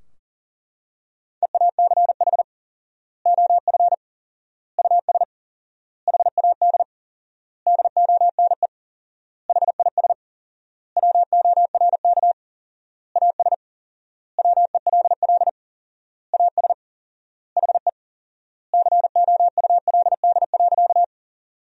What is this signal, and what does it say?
Morse code, heard as a keyed tone, about 40 wpm, 700 Hz. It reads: EACH OF US HAD DONE HIS WORK AS WELL AS HE COULD.
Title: 4x Word-Spacing - Sentences from Top 500 Words - 40wpm